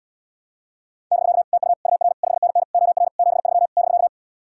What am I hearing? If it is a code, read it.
Morse code, heard as a keyed tone, about 77 wpm, 700 Hz. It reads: <BK> TU UR 5NN OR 73 <BK>